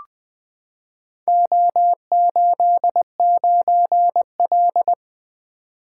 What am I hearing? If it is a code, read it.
Morse code, heard as a keyed tone, about 21 wpm, 700 Hz. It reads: O89L
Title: Four Letter-Number Combinations - 20wpm